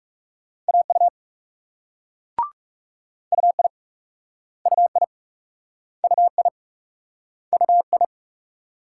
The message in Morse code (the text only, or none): AU UI UI UI UI